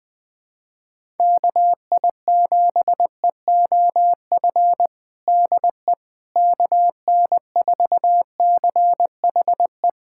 Morse code, heard as a keyed tone, about 21 wpm, 700 Hz. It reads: KI7EOF DE KN4CHE
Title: USA Callsign DE Callsign - 6 Characters - 20wpm